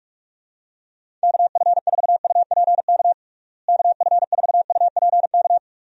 Morse code, heard as a keyed tone, about 43 wpm, 700 Hz. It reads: KF4UPK KF4UPK
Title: USA Call Signs - 6 Characters - Repeated 3x - 45wpm